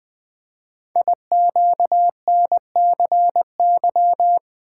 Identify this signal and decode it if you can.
Morse code, heard as a keyed tone, about 20 wpm, 700 Hz. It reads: IQNCY
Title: Code Groups - 5 Letters - 20wpm